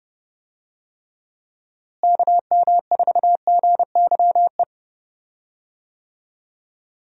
Morse code, heard as a keyed tone, about 31 wpm, 700 Hz. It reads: KM4GYE
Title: USA Call Signs - 6 Characters - Repeated 3x - 30wpm